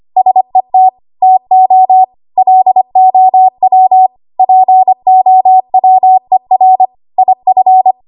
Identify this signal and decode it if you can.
Morse code, heard as a keyed tone, about 25 wpm, 750 Hz. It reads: SET TO LOW POWER IF
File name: Archive / 25 WPM / 25WPM